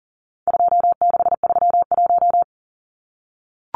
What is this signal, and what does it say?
Morse code, heard as a keyed tone, about 40 wpm, 700 Hz. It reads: 2631
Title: Numbers - 4-Digit 40wpm